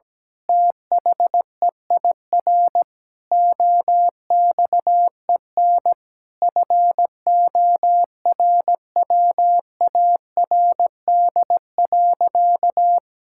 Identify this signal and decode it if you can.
Morse code, heard as a keyed tone, about 17 wpm, 700 Hz. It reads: THEIR OXEN FORWARD.
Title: Pangram Challenge v2 - 17wpm